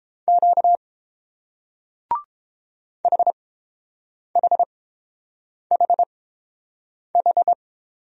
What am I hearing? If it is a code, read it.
Morse code, heard as a keyed tone, about 30 wpm, spaced as for 7 wpm, 700 Hz. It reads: Q HHHH